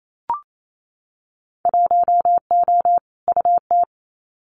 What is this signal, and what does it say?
Morse code, heard as a keyed tone, about 28 wpm, 700 Hz. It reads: 1O UT
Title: Winter Field Day Exchanges Encore 28wpm